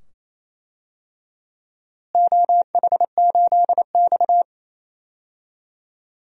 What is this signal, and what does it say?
Morse code, heard as a keyed tone, about 28 wpm, 700 Hz. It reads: OH8X
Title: DXCC-100 Call Signs - 4 Characters - Repeated 2x - 28wpm